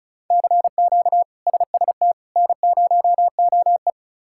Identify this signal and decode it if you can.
Morse code, heard as a keyed tone, about 34 wpm, 700 Hz. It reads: CQ SST N0OE
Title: K1USN SST QSO Practice - 35wpm